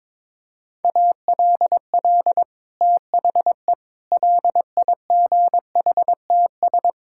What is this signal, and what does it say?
Morse code, heard as a keyed tone, about 23 wpm, 700 Hz. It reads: ALL THE LIGHTS
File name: practice / 22wpm / Sentences with 2-Word Phrases - 22wpm